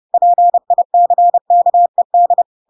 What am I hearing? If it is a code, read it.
Morse code, heard as a keyed tone, about 30 wpm, 700 Hz. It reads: PICKED
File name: Pinocchio / Pinocchio ch1 30 30 ik1hge